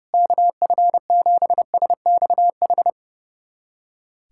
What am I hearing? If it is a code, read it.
Morse code, heard as a keyed tone, about 32 wpm, 700 Hz. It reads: KF7SXH
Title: USA Callsign DE Callsign - 6 Characters - 30wpm